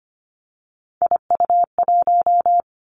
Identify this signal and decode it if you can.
Morse code, heard as a keyed tone, about 25 wpm, 700 Hz. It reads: IU1
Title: Three Challenging Letter-Number Combinations 25wpm